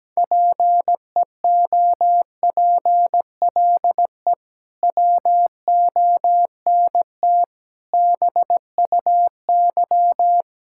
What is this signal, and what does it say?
Morse code, heard as a keyed tone, about 17 wpm, 700 Hz. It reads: PEOPLE WONT BUY
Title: Sentences from Top 1000 Words - Review Part 6 - 17wpm